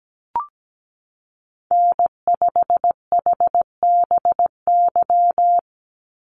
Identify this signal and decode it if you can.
Morse code, heard as a keyed tone, about 17 wpm, 700 Hz. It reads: N5HBY
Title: USA Call Signs 17wpm